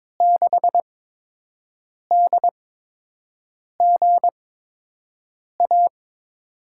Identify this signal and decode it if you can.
Morse code, heard as a keyed tone, about 22 wpm, 700 Hz. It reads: 6 D G A
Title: USA Call Signs - 6 Characters - 8x Character Spacing - 22wpm